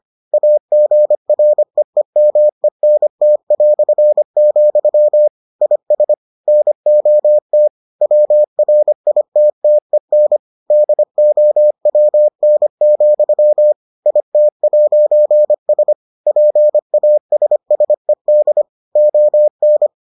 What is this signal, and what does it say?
Morse code, heard as a keyed tone, about 25 wpm, 600 Hz. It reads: AGREEMENT", IS NOT WRITTEN DOWN, IT'S PASSED ON